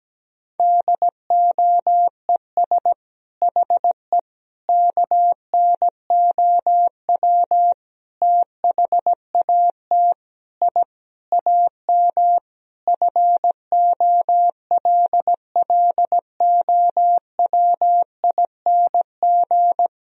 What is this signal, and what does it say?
Morse code, heard as a keyed tone, about 17 wpm, 700 Hz. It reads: DOES HE KNOW THAT I AM FOLLOWING
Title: Sentences from Top 1000 Words - Review Part 4 - 17wpm